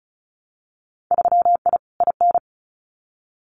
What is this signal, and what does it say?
Morse code, heard as a keyed tone, about 34 wpm, 700 Hz. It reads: 3I IN
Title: Winter Field Day Exchanges 35wpm